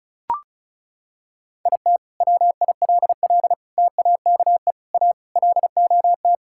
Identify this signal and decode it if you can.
Morse code, heard as a keyed tone, about 36 wpm, 700 Hz. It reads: IT WILL TAKE A LOT
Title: Sentences from Top 1000 Words - Review Part 6 - 35wpm